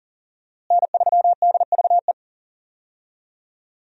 Morse code, heard as a keyed tone, about 40 wpm, 700 Hz. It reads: N3DVE
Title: USA Call Signs - 5 Characters - Repeated 3x - 40wpm